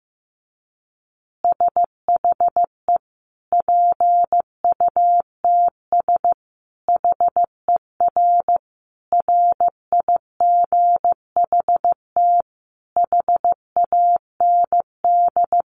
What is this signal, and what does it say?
Morse code, heard as a keyed tone, about 15 wpm, 700 Hz. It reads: SHE PUTS HER RIGHT HAND
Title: Easy Phrases 15wpm